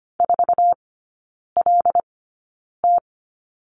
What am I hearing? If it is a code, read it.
Morse code, heard as a keyed tone, about 25 wpm, spaced as for 9 wpm, 690 Hz. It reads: 4LT
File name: Lesson 14e - Number 2 - Three Characters 25wpm